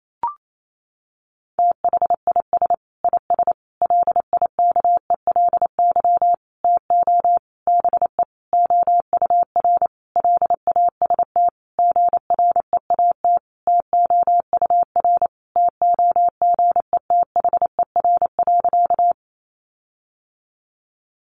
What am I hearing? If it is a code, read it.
Morse code, heard as a keyed tone, about 28 wpm, 700 Hz. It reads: THIS IS LIKELY TO BE OUR LAST GREAT TOUR TOGETHER.
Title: Sentences from Top 1400 Words 28wpm